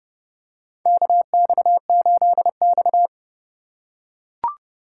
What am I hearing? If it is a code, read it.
Morse code, heard as a keyed tone, about 32 wpm, 700 Hz. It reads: KX8X
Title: K1USN SST QSO Practice - 30wpm